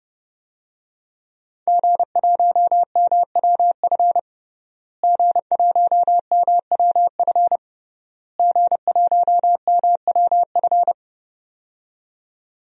Morse code, heard as a keyed tone, about 29 wpm, 700 Hz. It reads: G1MWF G1MWF G1MWF
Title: DXCC-100 Call Signs - 5 Characters - Repeated 3x - 30wpm